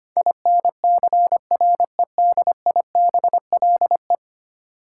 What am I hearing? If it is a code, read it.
Morse code, heard as a keyed tone, about 25 wpm, 700 Hz. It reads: INCREDIBLE
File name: 10 Letter Words - First 3 Letters Given - 25wpm